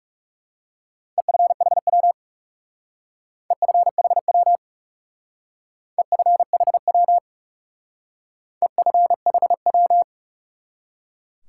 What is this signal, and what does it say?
Morse code, heard as a keyed tone, about 36 wpm, 700 Hz. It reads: EFHW EFHW EFHW EFHW
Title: Speed-Racing Top 180 QSO Elements 30wpm